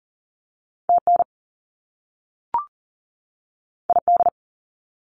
Morse code, heard as a keyed tone, about 40 wpm, 700 Hz. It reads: TN ID
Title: US States 40wpm